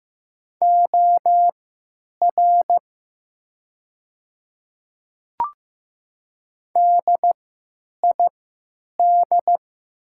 Morse code, heard as a keyed tone, about 15 wpm, spaced as for 8 wpm, 700 Hz. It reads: OR DID
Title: 3x Character Spacing - Top 100 Words - 15wpm